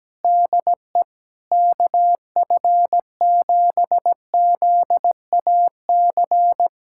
Morse code, heard as a keyed tone, about 17 wpm, 700 Hz. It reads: DE KF7ZAC
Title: USA Callsign DE Callsign - 6 Characters - 17wpm